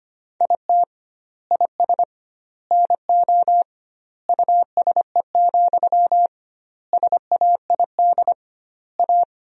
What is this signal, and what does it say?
Morse code, heard as a keyed tone, about 25 wpm, 700 Hz. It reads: IT IS NO USE, SAID A
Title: 2x Word-Spacing - Sentences from Top 500 Words - 25wpm